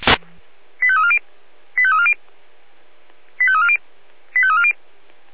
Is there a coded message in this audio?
Yes: five-tone selective call (ZVEI) from 0.8 to 4.7 s, heard as quick runs of five tones, tones of about 70 ms. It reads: 85340 85340 85340 85340